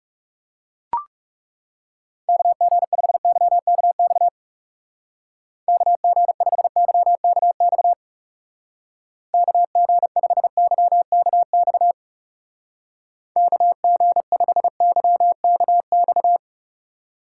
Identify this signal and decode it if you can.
Morse code, heard as a keyed tone, about 39 wpm, 700 Hz. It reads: KG5YKX KG5YKX KG5YKX KG5YKX